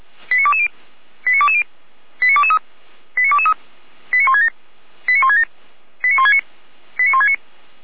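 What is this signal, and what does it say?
Five-tone selective call (ZVEI), heard as quick runs of five tones, tones of about 70 ms. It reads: 89220 89220 89223 89223 89167 89167 89169 89169
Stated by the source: ZVEI five-tone call 89022 received by SDR on the 455 channel, feuerwehr-orlamuende.org fire brigade audio server.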